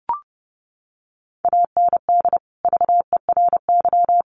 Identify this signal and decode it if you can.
Morse code, heard as a keyed tone, about 30 wpm, 700 Hz. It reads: AND VERY
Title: Sentences from Top 1300 Words 30wpm